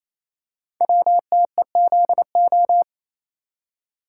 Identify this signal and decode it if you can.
Morse code, heard as a keyed tone, about 28 wpm, 700 Hz. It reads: WTEZO
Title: Code Groups - 5 Letters - 28wpm